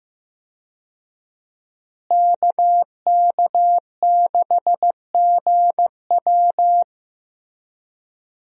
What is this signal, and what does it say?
Morse code, heard as a keyed tone, about 15 wpm, 700 Hz. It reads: KK6GW
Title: USA Call Signs - 5 Characters - Repeated 3x - 15wpm